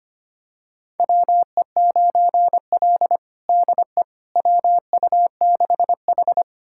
Morse code, heard as a keyed tone, about 25 wpm, 700 Hz. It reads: WE9L DE WU6H
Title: USA Callsign DE Callsign - 4 Characters - 25wpm